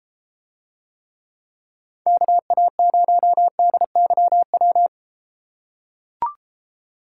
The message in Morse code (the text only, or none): KA0DYW